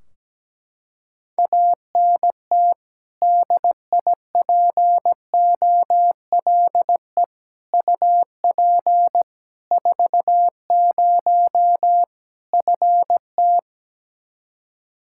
Morse code, heard as a keyed tone, about 17 wpm, 700 Hz. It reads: ANT DIPOLE UP 40 FT
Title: Practice QSO - Exchange 2 - Info - 17wpm